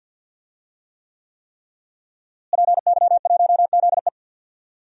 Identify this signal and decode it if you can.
Morse code, heard as a keyed tone, about 46 wpm, 700 Hz. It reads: PY1ZE
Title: DXCC-100 Call Signs - 5 Characters - Repeated 3x - 50wpm